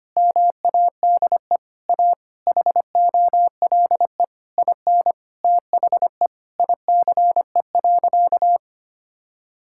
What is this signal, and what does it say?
Morse code, heard as a keyed tone, about 25 wpm, 700 Hz. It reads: MADE A HOLE IN THE ICE.
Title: Sentences from Top 1600 Words 25wpm